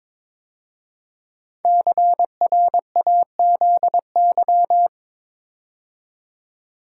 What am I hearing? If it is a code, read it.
Morse code, heard as a keyed tone, about 22 wpm, 700 Hz. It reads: CRAZY
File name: 5 Letter Words - First 3 Letters Given - 22wpm